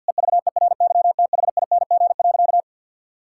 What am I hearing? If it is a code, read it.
Morse code, heard as a keyed tone, about 50 wpm, 700 Hz. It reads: EVERYTHING.
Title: Sentences from Top 1000 Words - Review Part 5 - 50wpm